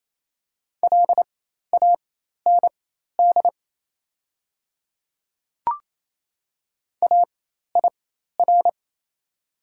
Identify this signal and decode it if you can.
Morse code, heard as a keyed tone, about 27 wpm, spaced as for 13 wpm, 700 Hz. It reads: LAND AIR
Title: 4x Character Spacing - Top 500 Words - 28wpm